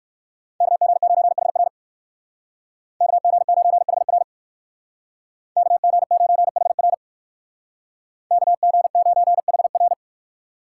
Morse code, heard as a keyed tone, about 59 wpm, 690 Hz. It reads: KG9HR KG9HR KG9HR KG9HR